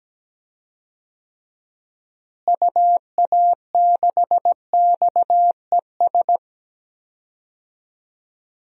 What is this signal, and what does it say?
Morse code, heard as a keyed tone, about 17 wpm, 700 Hz. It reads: UA6XES
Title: DXCC-100 Call Signs - 6 Characters - Repeated 2x - 17wpm